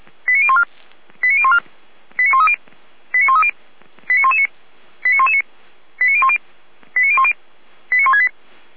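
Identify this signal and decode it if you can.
Five-tone selective call (ZVEI), heard as quick runs of five tones, tones of about 70 ms. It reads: 89014 89014 89130 89130 89119 89119 89010 89010 89168
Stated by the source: ZVEI five-tone selective call received by SDR.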